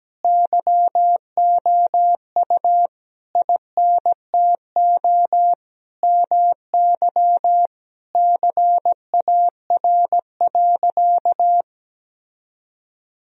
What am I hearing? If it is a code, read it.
Morse code, heard as a keyed tone, about 17 wpm, 700 Hz. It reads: YOU INTO MY CAR.
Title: Sentences from Top 1000 Words - Review Part 1 - 17wpm